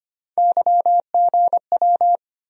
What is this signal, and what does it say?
Morse code, heard as a keyed tone, about 25 wpm, 700 Hz. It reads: YGW
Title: USA Call Sign Postfixes - 3 Letters 25wpm